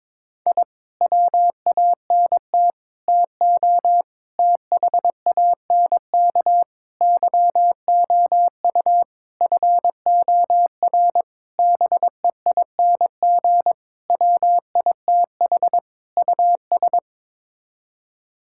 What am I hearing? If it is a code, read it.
Morse code, heard as a keyed tone, about 22 wpm, 700 Hz. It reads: I WANT TO THANK YOU FOR BEING WITH US
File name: Sets of 9 Words - 22wpm